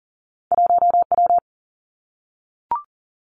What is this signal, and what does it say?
Morse code, heard as a keyed tone, about 39 wpm, 700 Hz. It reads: 1W
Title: Two Challenging Letter-Number Combinations 40wpm